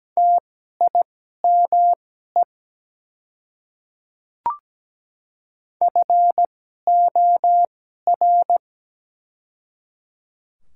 Morse code, heard as a keyed tone, about 17 wpm, spaced as for 12 wpm, 700 Hz. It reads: TIME FOR
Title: 2x Character Spacing - Top 100 Words - 17wpm